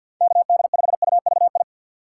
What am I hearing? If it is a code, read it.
Morse code, heard as a keyed tone, about 46 wpm, 700 Hz. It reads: KD5UVI
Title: USA Call Signs - 6 Characters - 3x Character Spacing - 50wpm